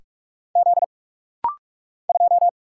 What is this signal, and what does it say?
Morse code, heard as a keyed tone, about 46 wpm, 700 Hz. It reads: Z 2
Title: Single Letter-Number - Rapid-Fire 45wpm